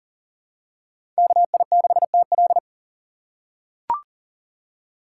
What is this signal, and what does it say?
Morse code, heard as a keyed tone, about 40 wpm, 700 Hz. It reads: KI6TL
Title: USA Call Signs - 4x Character Spacing - 40wpm